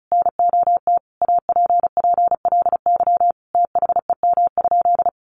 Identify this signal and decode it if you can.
Morse code, heard as a keyed tone, about 34 wpm, 700 Hz. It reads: NOT APPLY THEM?
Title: Sentences from Top 1400 Words 35wpm